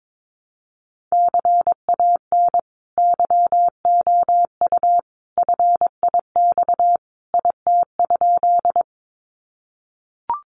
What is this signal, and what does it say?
Morse code, heard as a keyed tone, about 22 wpm, 700 Hz. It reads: CAN YOU FIX IT?
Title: Sentences from Top 1400 Words 22wpm